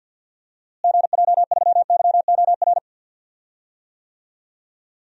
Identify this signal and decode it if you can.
Morse code, heard as a keyed tone, about 52 wpm, 700 Hz. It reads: GJ3YOR
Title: DXCC-100 Call Signs - 6 Characters - Repeated 3x - 50wpm